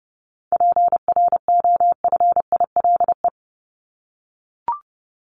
Morse code, heard as a keyed tone, about 30 wpm, 700 Hz. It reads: PROFILE
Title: Top 1501-1750 Words 30wpm